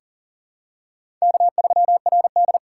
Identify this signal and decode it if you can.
Morse code, heard as a keyed tone, about 40 wpm, 700 Hz. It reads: K3RD
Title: USA Call Signs - 4x Character Spacing - 40wpm